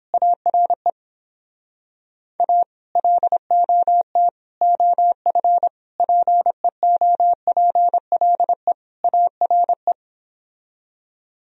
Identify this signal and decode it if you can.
Morse code, heard as a keyed tone, about 29 wpm, 700 Hz. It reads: ARE A LOT OF PEOPLE ARE